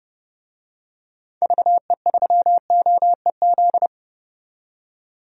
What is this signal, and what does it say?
Morse code, heard as a keyed tone, about 30 wpm, 700 Hz. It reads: VE3OEZ
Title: DXCC-100 Call Signs - 6 Characters - 30wpm